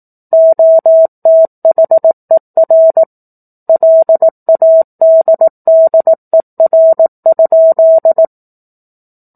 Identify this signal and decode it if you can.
Morse code, heard as a keyed tone, about 18 wpm, 650 Hz. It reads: OTHER LADDER?